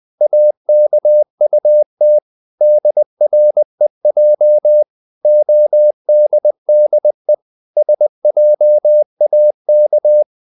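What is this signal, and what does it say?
Morse code, heard as a keyed tone, about 20 wpm, 600 Hz. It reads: AKUT DREJ ODDE SJAK
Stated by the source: Morse code practice of short Danish words at 20 wpm, from CW med Gnister.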